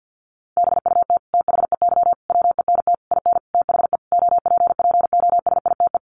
Morse code, heard as a keyed tone, about 50 wpm, 700 Hz. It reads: BUT THEY WENT IN THE OPPOSITE
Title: Sentences from Top 1300 Words 50wpm